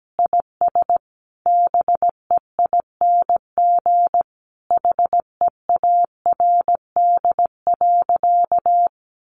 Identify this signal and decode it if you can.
Morse code, heard as a keyed tone, about 17 wpm, 700 Hz. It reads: IS BEING HEARD.
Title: Sentences from Top 1500 Words 17wpm